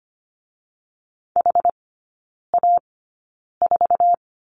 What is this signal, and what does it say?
Morse code, heard as a keyed tone, about 25 wpm, spaced as for 9 wpm, 690 Hz. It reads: HA4
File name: Lesson 14e - Number 2 - Three Characters 25wpm